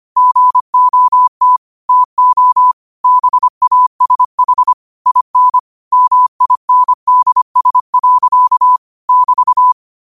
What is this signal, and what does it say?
Morse code, heard as a keyed tone, about 25 wpm, 1000 Hz. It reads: GOT TO BASH IN MINDS. <BT>
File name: qotdfiles / 25wpm